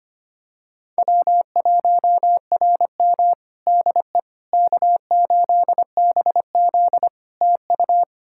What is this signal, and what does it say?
Morse code, heard as a keyed tone, about 25 wpm, 700 Hz. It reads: W1RM DE K8BZ TU
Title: Practice QSO - Exchange 4 - Ending - 25wpm